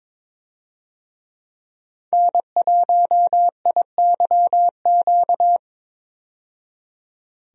Morse code, heard as a keyed tone, about 22 wpm, 700 Hz. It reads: N1IYQ
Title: USA Call Signs - 5 Characters - 22wpm